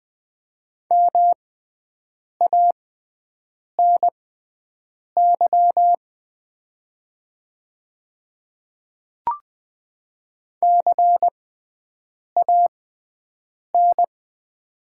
Morse code, heard as a keyed tone, about 20 wpm, spaced as for 7 wpm, 700 Hz. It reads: MANY CAN
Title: 6x Character Spacing - Top 500 Words - 20wpm